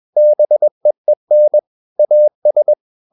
Morse code, heard as a keyed tone, about 21 wpm, 600 Hz. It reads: BEEN AS
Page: CW med Gnister